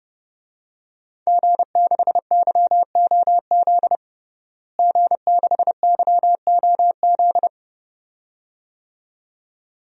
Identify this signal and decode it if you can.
Morse code, heard as a keyed tone, about 30 wpm, 700 Hz. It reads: G6YOZ G6YOZ
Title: DXCC-100 Call Signs - 5 Characters - Repeated 2x - 30wpm